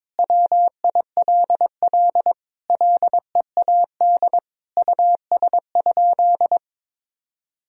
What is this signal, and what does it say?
Morse code, heard as a keyed tone, about 21 wpm, 700 Hz. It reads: WILL LEAD US?
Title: Sentences from Top 1000 Words - Review Part 2 - 22wpm